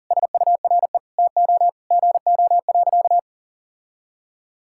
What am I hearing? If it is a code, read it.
Morse code, heard as a keyed tone, about 40 wpm, 700 Hz. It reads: SURE TO GO.
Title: Sentences from Top 1600 Words 40wpm